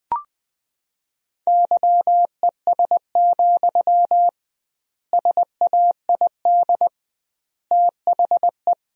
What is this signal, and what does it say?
Morse code, heard as a keyed tone, about 21 wpm, 700 Hz. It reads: YES, SAID THE
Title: 2x Word-Spacing - Sentences from Top 500 Words - 20wpm